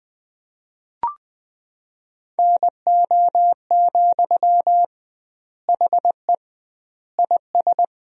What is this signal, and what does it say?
Morse code, heard as a keyed tone, about 20 wpm, 700 Hz. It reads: NO, HE IS
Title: 2x Word-Spacing - Sentences from Top 400 Words - 20wpm